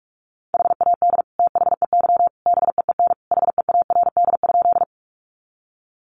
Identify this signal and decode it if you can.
Morse code, heard as a keyed tone, about 44 wpm, 700 Hz. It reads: HAD THEY BEEN HEARD?
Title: Sentences from Top 400 Words 45wpm